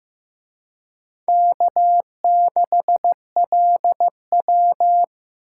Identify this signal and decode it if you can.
Morse code, heard as a keyed tone, about 18 wpm, 700 Hz. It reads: K6LW
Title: Speed Racing - USA 1x2 Call Signs 15wpm